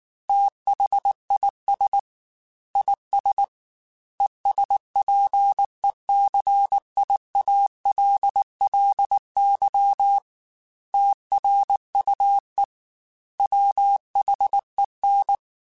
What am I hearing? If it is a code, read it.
Morse code, heard as a keyed tone, about 19 wpm, 800 Hz. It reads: THIS IS ESPECIALLY TRUE WHEN
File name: Technical Subjects / Technical Subjects-mp3s / AND Gate @18